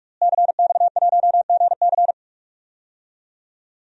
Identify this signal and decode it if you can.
Morse code, heard as a keyed tone, about 46 wpm, 700 Hz. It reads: CX1GC
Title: DXCC-100 Call Signs - 5 Characters - Repeated 3x - 45wpm